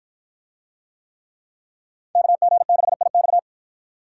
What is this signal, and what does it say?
Morse code, heard as a keyed tone, about 49 wpm, 700 Hz. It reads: KG6IX